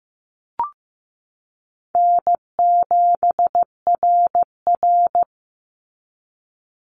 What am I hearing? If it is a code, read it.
Morse code, heard as a keyed tone, about 15 wpm, 700 Hz. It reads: N7RR
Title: NA SOTA Activator Callsigns 15wpm